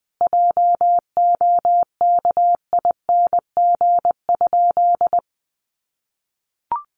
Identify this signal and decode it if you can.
Morse code, heard as a keyed tone, about 20 wpm, 700 Hz. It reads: JOKING?
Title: A Sound Of Thunder 20wpm